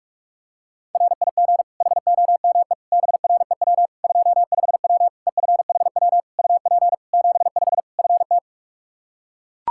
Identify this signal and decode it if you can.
Morse code, heard as a keyed tone, about 41 wpm, 700 Hz. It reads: RIG HOME BREW 25W EFHW UP 75 FT
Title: K1USN SST QSO Practice 45wpm